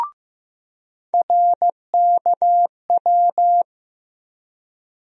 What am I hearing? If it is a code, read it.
Morse code, heard as a keyed tone, about 15 wpm, 700 Hz. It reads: RKW
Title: USA Call Sign Postfixes - 3 Letters 15wpm